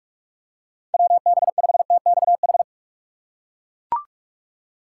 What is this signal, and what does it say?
Morse code, heard as a keyed tone, about 41 wpm, 700 Hz. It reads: WB5TXH
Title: USA Call Signs - 2x Character Spacing - 45wpm